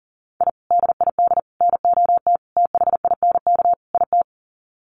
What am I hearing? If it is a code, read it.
Morse code, heard as a keyed tone, about 40 wpm, 700 Hz. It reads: I DID NOT THINK IT
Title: Sets of 5 Words Encore2 40wpm